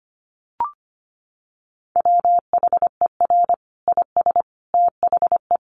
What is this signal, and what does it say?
Morse code, heard as a keyed tone, about 25 wpm, 700 Hz. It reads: WHER IS THE
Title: Sentences from Top 1500 Words 25wpm